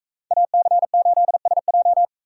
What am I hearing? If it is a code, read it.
Morse code, heard as a keyed tone, about 41 wpm, 700 Hz. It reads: AC8SJ